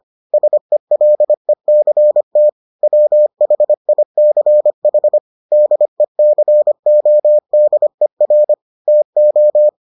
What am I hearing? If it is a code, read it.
Morse code, heard as a keyed tone, about 25 wpm, 600 Hz. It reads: SELECT WHICH DECODER TO